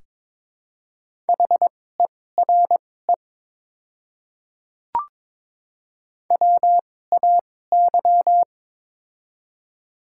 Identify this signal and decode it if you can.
Morse code, heard as a keyed tone, about 21 wpm, spaced as for 16 wpm, 700 Hz. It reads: HERE WAY
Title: 2x Character Spacing - Top 300 Words - 22wpm